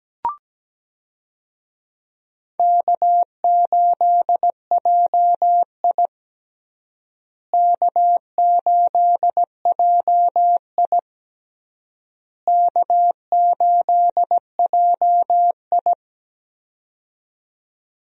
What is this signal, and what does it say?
Morse code, heard as a keyed tone, about 17 wpm, 700 Hz. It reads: K8JI K8JI K8JI
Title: USA Call Signs - 4 Characters - Repeated 3x - 17wpm